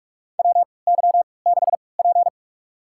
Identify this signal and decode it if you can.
Morse code, heard as a keyed tone, about 43 wpm, spaced as for 25 wpm, 700 Hz. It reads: WY6P
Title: USA Call Signs - 4 Characters - 3x Character Spacing - 45wpm